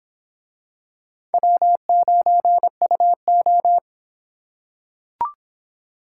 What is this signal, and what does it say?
Morse code, heard as a keyed tone, about 25 wpm, 700 Hz. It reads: W9UO